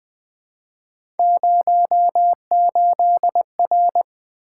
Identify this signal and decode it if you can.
Morse code, heard as a keyed tone, about 20 wpm, 700 Hz. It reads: 08R
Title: Three Letter-Number Combinations - 20wpm